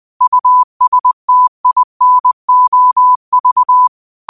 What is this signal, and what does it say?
Morse code, heard as a keyed tone, about 20 wpm, 1000 Hz. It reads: USTINOV
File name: qotdfiles / 20wpm